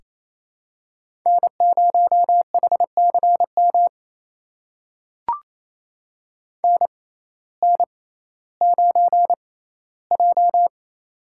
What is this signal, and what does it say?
Morse code, heard as a keyed tone, about 28 wpm, spaced as for 15 wpm, 700 Hz. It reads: N0HCM NN9J
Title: USA Call Signs - 6x Character Spacing - 28wpm